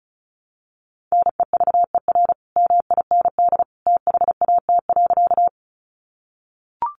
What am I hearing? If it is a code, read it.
Morse code, heard as a keyed tone, about 34 wpm, 700 Hz. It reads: NEVER MIND THAT.
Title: Sentences from Top 500 Words 35wpm